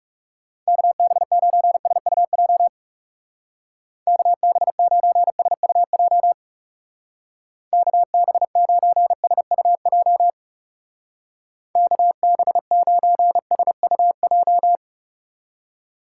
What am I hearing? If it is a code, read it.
Morse code, heard as a keyed tone, about 38 wpm, 700 Hz. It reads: KB9SUJ KB9SUJ KB9SUJ KB9SUJ